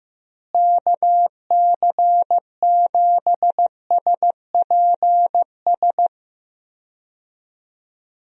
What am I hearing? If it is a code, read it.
Morse code, heard as a keyed tone, about 15 wpm, 700 Hz. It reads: KC7SPS
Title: USA Call Signs - 6 Characters - Repeated 3x - 15wpm